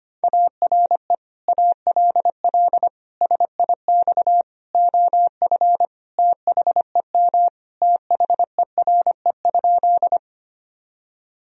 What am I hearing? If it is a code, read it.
Morse code, heard as a keyed tone, about 25 wpm, 700 Hz. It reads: ARE ALL SIX OF THEM THERE?
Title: Sentences from Top 1000 Words - Review Part 2 - 25wpm